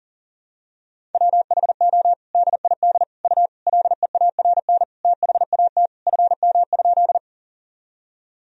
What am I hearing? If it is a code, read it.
Morse code, heard as a keyed tone, about 38 wpm, 700 Hz. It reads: WHO DID U LEARN THAT FM?
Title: Sentences from Top 300 Words and Abbreviations 40wpm